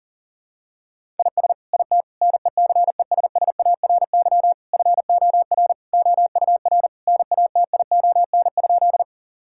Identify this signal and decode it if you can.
Morse code, heard as a keyed tone, about 40 wpm, 700 Hz. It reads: IS IT NECESSARY FOR OUR NATION?
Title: Sentences from Top 1000 Words - Review Part 6 - 40wpm